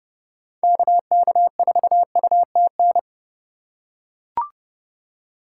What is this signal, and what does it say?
Morse code, heard as a keyed tone, about 30 wpm, 700 Hz. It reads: KK4UTN
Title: USA Call Signs - 6 Characters - 4x Character Spacing - 30wpm